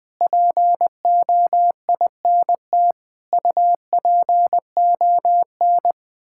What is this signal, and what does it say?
Morse code, heard as a keyed tone, about 20 wpm, 700 Hz. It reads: POINT UPON
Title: Sentences from Top 1600 Words 20wpm